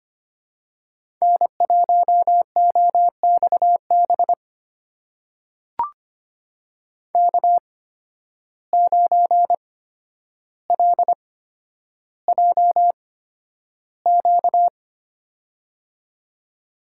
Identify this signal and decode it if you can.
Morse code, heard as a keyed tone, about 25 wpm, spaced as for 11 wpm, 700 Hz. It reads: N1OXB K9LJQ
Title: USA Call Signs - 5 Characters - 8x Character Spacing - 25wpm